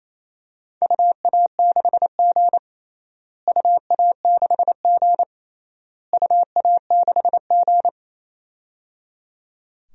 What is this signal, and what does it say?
Morse code, heard as a keyed tone, about 27 wpm, 700 Hz. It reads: UA6G UA6G UA6G
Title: DXCC-100 Call Signs - 4 Characters - Repeated 3x - 28wpm